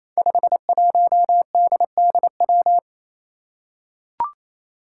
Morse code, heard as a keyed tone, about 28 wpm, 700 Hz. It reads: 51DDW
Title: Code Groups - 5 Letter:Numbers - 28wpm